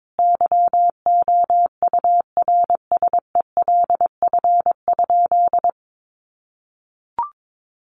Morse code, heard as a keyed tone, about 22 wpm, 700 Hz. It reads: YOURSELF?
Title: Sentences from Top 1500 Words 22wpm